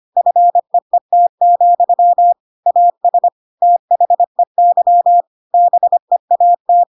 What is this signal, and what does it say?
Morse code, heard as a keyed tone, about 25 wpm, 700 Hz. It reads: FEET, AS THEY BEAT
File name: Pinocchio / Pinocchio ch3 25 25 ik1hge